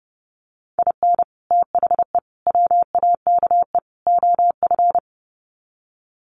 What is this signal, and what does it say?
Morse code, heard as a keyed tone, about 30 wpm, 700 Hz. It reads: IN THE WAKE OF
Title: Sets of 4 Words 30wpm